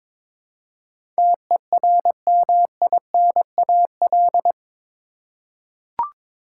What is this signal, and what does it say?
Morse code, heard as a keyed tone, about 22 wpm, 700 Hz. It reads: TERMINAL
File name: practice / 22wpm / Eight Letter Words - 22wpm